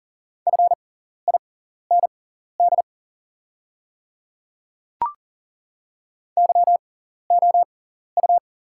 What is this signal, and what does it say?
Morse code, heard as a keyed tone, about 40 wpm, spaced as for 14 wpm, 690 Hz. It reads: FIND YOU
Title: 6x Character Spacing - Top 100 Words - 40wpm